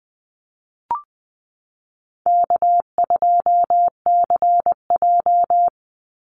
Morse code, heard as a keyed tone, about 19 wpm, 700 Hz. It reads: K2CJ
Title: Easy Call Signs Repeated 2x - 20wpm